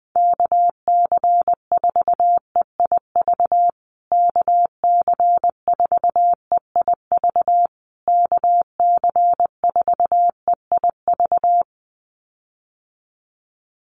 Morse code, heard as a keyed tone, about 20 wpm, 700 Hz. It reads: KC4EIV KC4EIV KC4EIV
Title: USA Call Signs Repeated 3x 20wpm